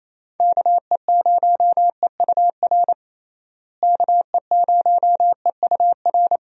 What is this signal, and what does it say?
Morse code, heard as a keyed tone, about 27 wpm, 700 Hz. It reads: KE0EUR KE0EUR
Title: USA Call Signs - 6 Characters - Repeated 2x - 28wpm